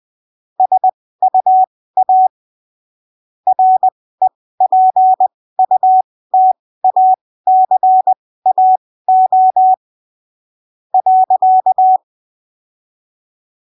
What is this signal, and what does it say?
Morse code, heard as a keyed tone, about 20 wpm, spaced as for 15 wpm, 750 Hz. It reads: SUA REPUTACAO .
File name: arquivos / texto / texto 28 20ppm